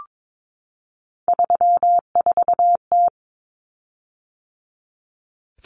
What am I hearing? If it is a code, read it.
Morse code, heard as a keyed tone, about 22 wpm, 700 Hz. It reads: 34T
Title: Common Cut Numbers - 3-Digit 22wpm